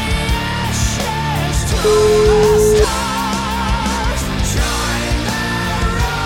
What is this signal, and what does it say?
Ringback tone (Europe, 425 Hz), with noise.